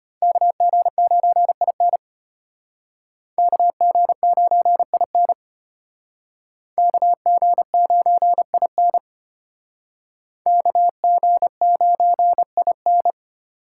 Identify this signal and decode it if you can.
Morse code, heard as a keyed tone, about 32 wpm, 700 Hz. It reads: KG9IN KG9IN KG9IN KG9IN